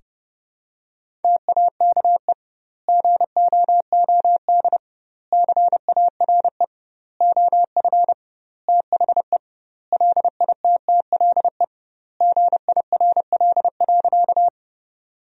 Morse code, heard as a keyed tone, about 32 wpm, 700 Hz. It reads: TAKE GOOD CARE OF THE LITTLE GIRL.
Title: 2x Word-Spacing - Sentences from Top 400 Words - 30wpm